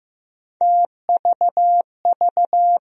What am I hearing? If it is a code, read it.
Morse code, heard as a keyed tone, about 15 wpm, 700 Hz. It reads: TVV
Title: USA Call Sign Postfixes - 3 Letters 15wpm